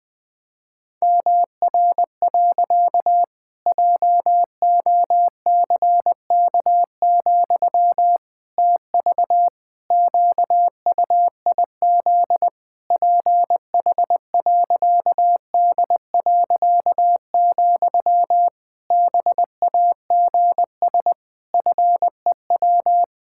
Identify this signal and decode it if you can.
Morse code, heard as a keyed tone, about 20 wpm, 700 Hz. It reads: MR. JOCK, TV QUIZ PH.D., BAGS FEW
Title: Pangram Challenge v2 - 20wpm